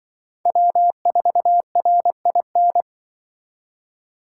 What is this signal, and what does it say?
Morse code, heard as a keyed tone, about 24 wpm, 700 Hz. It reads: W4RIN
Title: Speed Racing - USA 1x3 Call Signs 20wpm